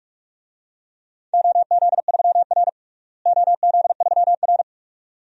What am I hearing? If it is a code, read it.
Morse code, heard as a keyed tone, about 46 wpm, 700 Hz. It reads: OZ3R OZ3R
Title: DXCC-100 Call Signs - 4 Characters - Repeated 2x - 45wpm